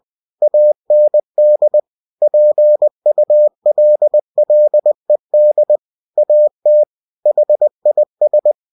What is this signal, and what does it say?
Morse code, heard as a keyed tone, about 20 wpm, 600 Hz. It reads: AND PULLED AT HIS
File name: cw-books / winnie-the-pooh / winnie-20wpm-page-0076